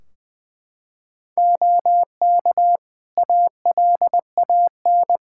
Morse code, heard as a keyed tone, about 21 wpm, 700 Hz. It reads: OK ALAN
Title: Practice QSO - Exchange 3 - More Info - 20wpm